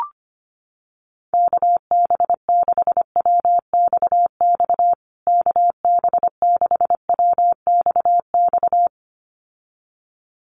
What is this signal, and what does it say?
Morse code, heard as a keyed tone, about 25 wpm, 700 Hz. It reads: KB6WXX KB6WXX
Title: USA Call Signs Repeated 2x 25wpm